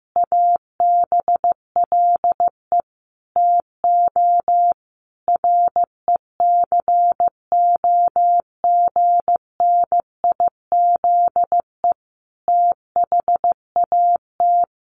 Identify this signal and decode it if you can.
Morse code, heard as a keyed tone, about 15 wpm, 700 Hz. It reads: ABLE TO RECOGNIZE THAT
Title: Sentences from Top 1400 Words 15wpm